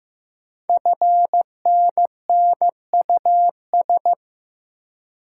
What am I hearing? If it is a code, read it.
Morse code, heard as a keyed tone, about 15 wpm, 700 Hz. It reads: FNNUS
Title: Code Groups - 5 Letters - 15wpm